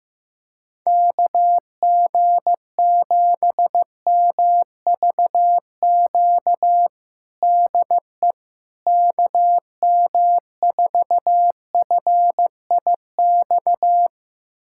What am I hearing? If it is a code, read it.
Morse code, heard as a keyed tone, about 15 wpm, 700 Hz. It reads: KG7MVQ DE KM4FIX
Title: USA Callsign DE Callsign 15wpm